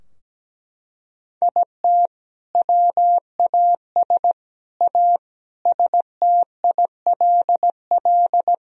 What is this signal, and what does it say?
Morse code, heard as a keyed tone, about 17 wpm, 700 Hz. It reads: IT WAS A STILL
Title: Sentences from Top 1000 Words - Review Part 2 - 17wpm